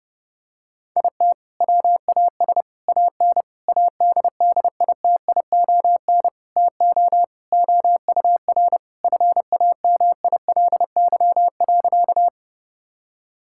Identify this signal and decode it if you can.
Morse code, heard as a keyed tone, about 30 wpm, 700 Hz. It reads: IT WAS AN ADDITION TO OUR FAMILY.
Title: Sentences from Top 1000 Words - Review Part 6 - 30wpm